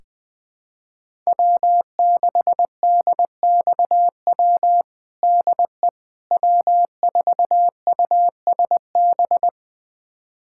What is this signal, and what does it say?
Morse code, heard as a keyed tone, about 21 wpm, 700 Hz. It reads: W6DXW DE W4USB
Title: USA Callsign DE Callsign - 5 Characters - 20wpm